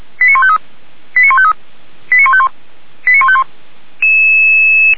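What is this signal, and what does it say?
Five-tone selective call (ZVEI), heard as quick runs of five tones, tones of about 70 ms. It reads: 89153 89153 89151 89151
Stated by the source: ZVEI five-tone selective call received on an SDR.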